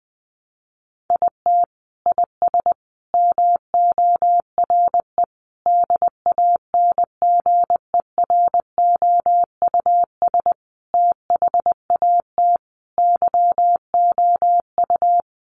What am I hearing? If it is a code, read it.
Morse code, heard as a keyed tone, about 20 wpm, 700 Hz. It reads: IT IS MORE DANGEROUS THAT YOU
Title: Sentences from Top 1200 Words 20wpm